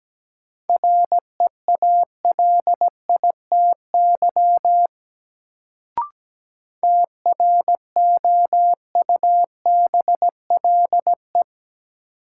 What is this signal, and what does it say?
Morse code, heard as a keyed tone, about 17 wpm, 700 Hz. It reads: REALITY TROUBLE
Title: Sentences from Top 1200 Words - New Word Drill Alt - 17wpm